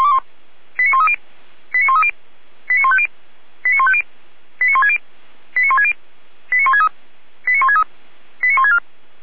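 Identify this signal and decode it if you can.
Five-tone selective call (ZVEI), heard as quick runs of five tones, tones of about 70 ms. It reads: 89130 89130 89150 89150 89160 89160 89163 89163 89164
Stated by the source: ZVEI five-tone selective call 89120 received with an SDR.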